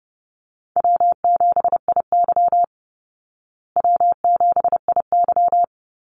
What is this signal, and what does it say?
Morse code, heard as a keyed tone, about 30 wpm, 700 Hz. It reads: W7IY W7IY
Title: Easy Call Signs Repeated 2x - 30wpm